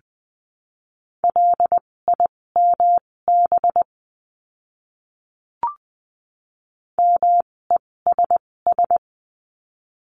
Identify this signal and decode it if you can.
Morse code, heard as a keyed tone, about 20 wpm, spaced as for 16 wpm, 700 Hz. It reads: LIMB MESS
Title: 256 Lesson 42a - Words 12-20wpm